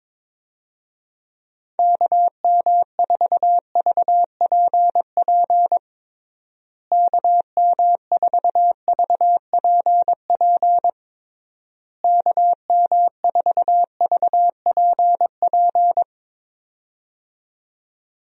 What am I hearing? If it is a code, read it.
Morse code, heard as a keyed tone, about 22 wpm, 700 Hz. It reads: KM4VPP KM4VPP KM4VPP
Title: USA Call Signs - 6 Characters - Repeated 3x - 22wpm